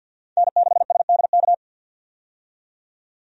Morse code, heard as a keyed tone, about 51 wpm, 700 Hz. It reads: N6SDK
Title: USA Call Signs - 5 Characters - 50wpm